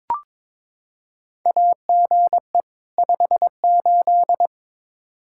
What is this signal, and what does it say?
Morse code, heard as a keyed tone, about 22 wpm, 700 Hz. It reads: AGE 58
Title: Practice QSO - Exchange 3 - More Info - 22wpm